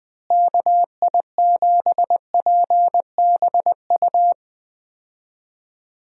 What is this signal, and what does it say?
Morse code, heard as a keyed tone, about 20 wpm, 700 Hz. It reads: KI7PBU
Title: USA Call Signs - 6 Characters - Repeated 3x - 20wpm